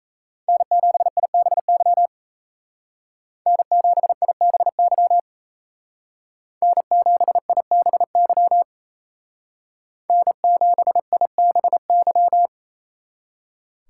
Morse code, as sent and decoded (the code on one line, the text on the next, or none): -. --... .. -... -.-- / -. --... .. -... -.-- / -. --... .. -... -.-- / -. --... .. -... -.--
N7IBY N7IBY N7IBY N7IBY